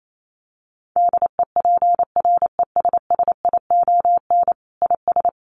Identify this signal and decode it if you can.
Morse code, heard as a keyed tone, about 28 wpm, 700 Hz. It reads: DEPRESSION IS
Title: Sentences from Top 1300 Words 28wpm